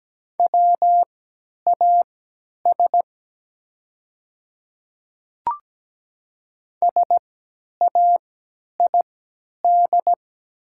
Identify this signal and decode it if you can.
Morse code, heard as a keyed tone, about 17 wpm, spaced as for 10 wpm, 700 Hz. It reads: WAS SAID
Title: 3x Character Spacing - Top 500 Words - 17wpm